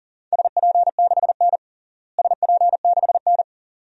Morse code, heard as a keyed tone, about 40 wpm, 700 Hz. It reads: SP6N SP6N
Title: DXCC-100 Call Signs - 4 Characters - Repeated 2x - 40wpm